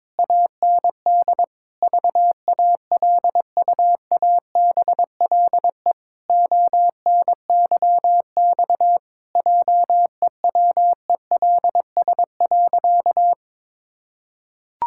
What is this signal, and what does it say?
Morse code, heard as a keyed tone, about 22 wpm, 700 Hz. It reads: AND VALUABLE ONYX JEWELS.
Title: Pangram Challenge v2 - 22wpm